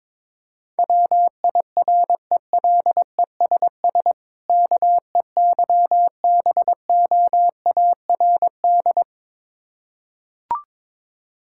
Morse code, heard as a keyed tone, about 22 wpm, 690 Hz. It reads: WIRELESS KEYBOARD
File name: Difficult Words with Hints - 22wpm